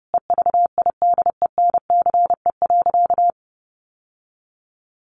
Morse code, heard as a keyed tone, about 30 wpm, 700 Hz. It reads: EVIDENCE.
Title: Sentences from Top 1300 Words 30wpm